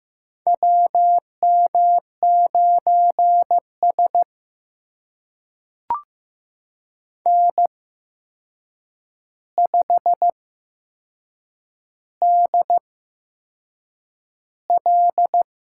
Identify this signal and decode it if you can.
Morse code, heard as a keyed tone, about 15 wpm, spaced as for 6 wpm, 700 Hz. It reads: WM9S N5DL
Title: USA Call Signs - 4 Characters - 8x Character Spacing - 15wpm